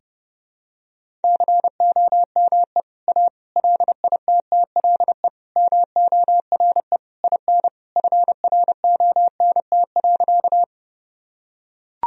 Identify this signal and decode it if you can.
Morse code, heard as a keyed tone, about 32 wpm, 700 Hz. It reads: COME A LITTLE MORE IN FRONT.
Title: Sentences from Top 1000 Words - Review Part 2 - 30wpm